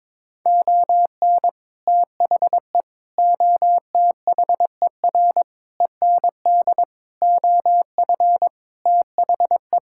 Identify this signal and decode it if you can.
Morse code, heard as a keyed tone, about 22 wpm, 700 Hz. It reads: ON THE OTHER END OF THE
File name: practice / 22wpm / Sets of 7 Words - 22wpm